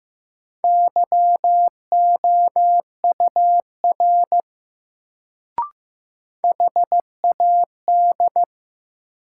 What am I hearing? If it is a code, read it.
Morse code, heard as a keyed tone, about 15 wpm, 700 Hz. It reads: YOUR HAD
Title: Sets of 2 Words - New Word Drill - 15wpm